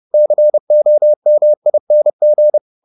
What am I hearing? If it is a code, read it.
Morse code, heard as a keyed tone, about 30 wpm, 600 Hz. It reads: COMING